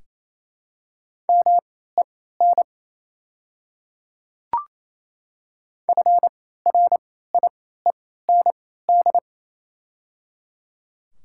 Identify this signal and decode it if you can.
Morse code, heard as a keyed tone, about 27 wpm, spaced as for 16 wpm, 700 Hz. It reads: MEN FRIEND
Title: 3x Character Spacing - Top 500 Words - 28wpm